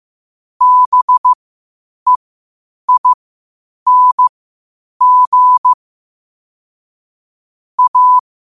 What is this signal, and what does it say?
Morse code, heard as a keyed tone, about 15 wpm, spaced as for 8 wpm, 1000 Hz. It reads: BEING A
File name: qotdfiles / 8wpm